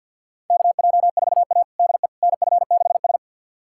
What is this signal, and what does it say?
Morse code, heard as a keyed tone, about 46 wpm, 700 Hz. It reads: KJ4A DE NF6S
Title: USA Callsign DE Callsign - 4 Characters - 50wpm